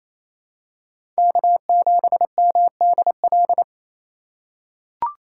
Morse code, heard as a keyed tone, about 28 wpm, 700 Hz. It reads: K7MDL
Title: USA Call Signs - 5 Characters - 2x Character Spacing - 28wpm